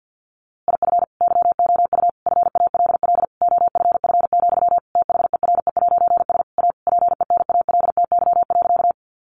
Morse code, heard as a keyed tone, about 49 wpm, 700 Hz. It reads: IF YOU FALL OFF, THERE'S A PENALTY.
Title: A Sound Of Thunder 50wpm